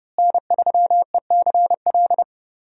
Morse code, heard as a keyed tone, about 30 wpm, 700 Hz. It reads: N3ECL
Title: USA Call Signs - 5 Characters - Repeated 3x - 30wpm